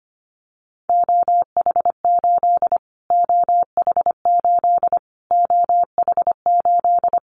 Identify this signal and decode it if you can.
Morse code, heard as a keyed tone, about 25 wpm, 700 Hz. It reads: OH8 OH8 OH8
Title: Callsign Prefixes for Top 25 DXCC - Repeated 3x - 25wpm